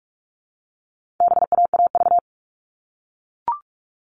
Morse code, heard as a keyed tone, about 44 wpm, 700 Hz. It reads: BAAV
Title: All Cut Numbers - 4-Digit 45wpm